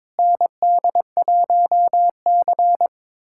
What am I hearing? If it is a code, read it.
Morse code, heard as a keyed tone, about 22 wpm, 700 Hz. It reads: ND1C
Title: USA Callsign DE Callsign - 4 Characters - 22wpm